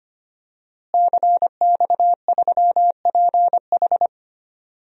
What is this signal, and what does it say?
Morse code, heard as a keyed tone, about 25 wpm, 700 Hz. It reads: CX3PH
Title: DXCC-100 Call Signs - 5 Characters - Repeated 3x - 25wpm